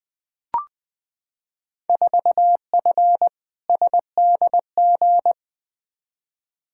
Morse code, heard as a keyed tone, about 20 wpm, 700 Hz. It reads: 4F SDG
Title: Field Day Exchanges - 20wpm